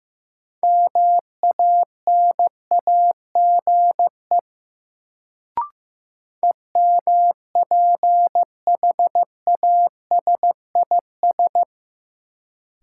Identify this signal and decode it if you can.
Morse code, heard as a keyed tone, about 15 wpm, 700 Hz. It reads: MANAGE EMPHASIS
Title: Sentences from Top 1400 Words - New Word Drill Alt - 15wpm